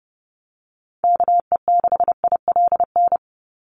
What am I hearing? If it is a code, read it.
Morse code, heard as a keyed tone, about 30 wpm, 700 Hz. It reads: KE6ILN
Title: USA Call Signs 30wpm